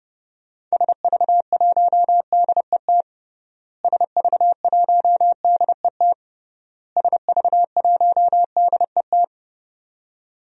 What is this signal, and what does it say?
Morse code, heard as a keyed tone, about 32 wpm, 700 Hz. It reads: SV1DET SV1DET SV1DET
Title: DXCC-100 Call Signs - 6 Characters - Repeated 3x - 30wpm